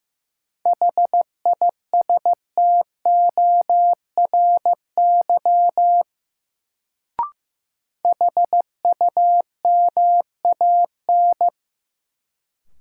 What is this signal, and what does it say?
Morse code, heard as a keyed tone, about 16 wpm, 700 Hz. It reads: HISTORY HUMAN
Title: Sentences from Top 600 Words - New Word Drill - 15wpm